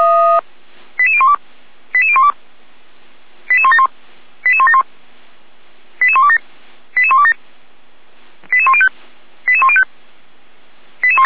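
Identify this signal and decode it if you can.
Five-tone selective call (ZVEI), heard as quick runs of five tones, tones of about 70 ms. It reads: 88012 88012 88171 88171 88127 88127 88105 88105
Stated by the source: ZVEI five-tone selective call 88203 received by SDR, logged by Feuerwehr Orlamünde.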